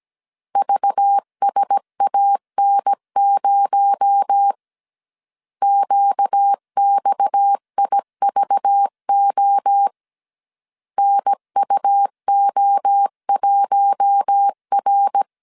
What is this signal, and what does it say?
Morse code, heard as a keyed tone, about 17 wpm, 800 Hz. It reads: VSAN0 QXIVO NUO1R
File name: 16wpm